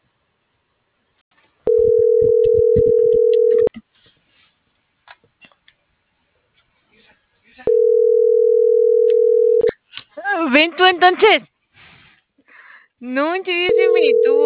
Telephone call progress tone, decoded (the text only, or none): ringback tone